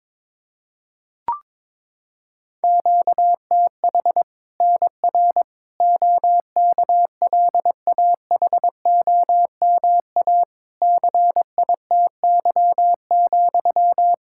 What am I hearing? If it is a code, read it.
Morse code, heard as a keyed tone, about 22 wpm, 700 Hz. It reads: QTH NR OKLAHOMA CITY,
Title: Practice QSO - Exchange 1 - Beginning - 22wpm